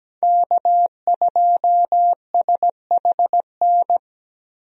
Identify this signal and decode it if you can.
Morse code, heard as a keyed tone, about 17 wpm, 700 Hz. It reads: K2SHN
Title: USA Call Signs - 5 Characters - Repeated 2x - 17wpm